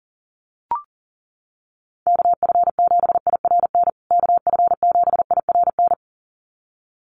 Morse code, heard as a keyed tone, about 40 wpm, 700 Hz. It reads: KF7IRN KF7IRN
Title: USA Call Signs Repeated 2x 40wpm